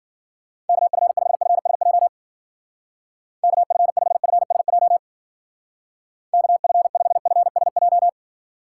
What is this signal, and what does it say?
Morse code, heard as a keyed tone, about 49 wpm, 700 Hz. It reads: KF5FSJ KF5FSJ KF5FSJ
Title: Speed Racing - USA 2x3 Call Signs 40wpm